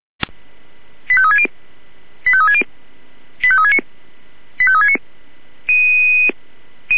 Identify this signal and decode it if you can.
Five-tone selective call (ZVEI), heard as quick runs of five tones, tones of about 70 ms. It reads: 85360 85360 85369 85369
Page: SDR/ZVEI